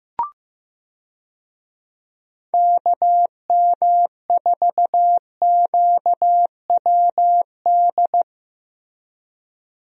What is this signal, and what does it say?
Morse code, heard as a keyed tone, about 15 wpm, 700 Hz. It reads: KM4QWD
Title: USA Call Signs - 6 Characters - Repeated 3x - 15wpm